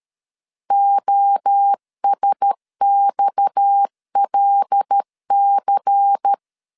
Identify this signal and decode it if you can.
Morse code, heard as a keyed tone, about 13 wpm, 800 Hz. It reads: OSXLC
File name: LEZIONI / 6 PKX0 / 6 PKX0 12wpm 2231